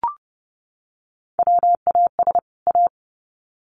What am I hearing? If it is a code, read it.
Morse code, heard as a keyed tone, about 28 wpm, 700 Hz. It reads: WAS A
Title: Sets of 2 Words 30wpm